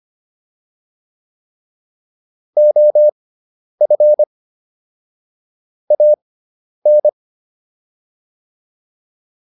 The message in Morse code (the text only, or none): OF AN